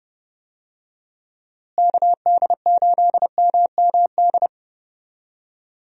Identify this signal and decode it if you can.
Morse code, heard as a keyed tone, about 30 wpm, 700 Hz. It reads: KD8MMD
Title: USA Call Signs - 6 Characters - Repeated 3x - 30wpm